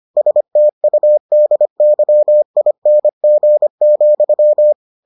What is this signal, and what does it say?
Morse code, heard as a keyed tone, about 25 wpm, 600 Hz. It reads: STUDYING,